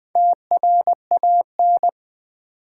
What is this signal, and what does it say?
Morse code, heard as a keyed tone, about 20 wpm, 700 Hz. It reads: TRAN
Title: Build A Word - 2 Letters At A Time - 8-Letter Words - 20wpm